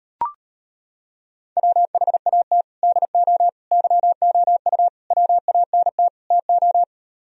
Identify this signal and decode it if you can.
Morse code, heard as a keyed tone, about 31 wpm, 700 Hz. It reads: WHAT DO YOU WANT TO
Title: Speed-Racing - Sets of 5 Words 25wpm